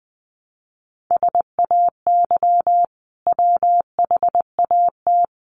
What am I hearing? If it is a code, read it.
Morse code, heard as a keyed tone, about 20 wpm, 700 Hz. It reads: SAY WHAT
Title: Sentences from Top 200 Words 20wpm